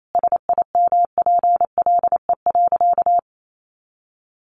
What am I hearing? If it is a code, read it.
Morse code, heard as a keyed tone, about 28 wpm, 700 Hz. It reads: SIMPLE.
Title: Sentences from Top 400 Words 28wpm